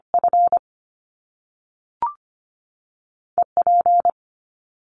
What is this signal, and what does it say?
Morse code, heard as a keyed tone, about 25 wpm, 700 Hz. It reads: F E P
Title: Top Two Letter Combinations in Words 25wpm